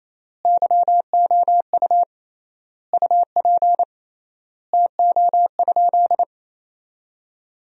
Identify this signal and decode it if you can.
Morse code, heard as a keyed tone, about 28 wpm, 700 Hz. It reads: YOU UP TO?
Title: 3x Word-Spacing - Sentences from Top 100 Words - 28wpm